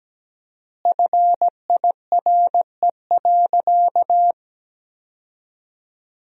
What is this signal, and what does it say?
Morse code, heard as a keyed tone, about 17 wpm, 700 Hz. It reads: FIRE.
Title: 4x Word-Spacing - Sentences from Top 400 Words - 17wpm